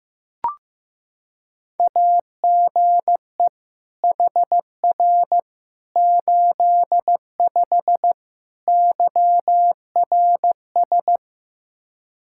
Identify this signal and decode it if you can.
Morse code, heard as a keyed tone, about 15 wpm, 700 Hz. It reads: AGE HR 85 YRS
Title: Practice QSO - Exchange 3 - More Info - 15wpm